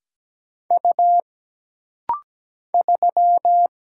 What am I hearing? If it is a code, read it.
Morse code, heard as a keyed tone, about 17 wpm, 700 Hz. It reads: U 3
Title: Single Letter-Number Alt - Rapid-Fire 17wpm